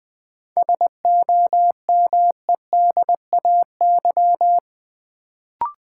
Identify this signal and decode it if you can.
Morse code, heard as a keyed tone, about 20 wpm, 700 Hz. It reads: SOMEDAY
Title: Sentences from Top 1600 Words - New Word Drill - 20wpm